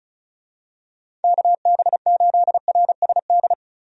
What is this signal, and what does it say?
Morse code, heard as a keyed tone, about 36 wpm, 700 Hz. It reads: KB8RSD
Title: USA Call Signs - 6 Characters - 35wpm